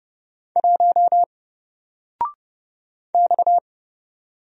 Morse code, heard as a keyed tone, about 31 wpm, 700 Hz. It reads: 1 X
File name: Single-Letter-Number-30wpm